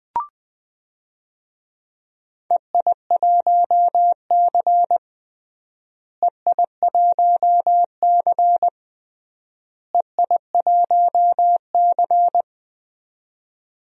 Morse code, heard as a keyed tone, about 21 wpm, 700 Hz. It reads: EI1C EI1C EI1C
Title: DXCC-100 Call Signs - 4 Characters - Repeated 3x - 20wpm